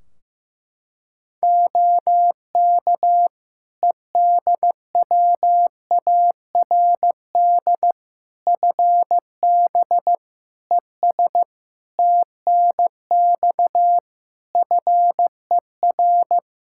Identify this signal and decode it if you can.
Morse code, heard as a keyed tone, about 15 wpm, 700 Hz. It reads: OK EDWARD FB ES TNX FER
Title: Practice QSO - Exchange 2 - Info - 15wpm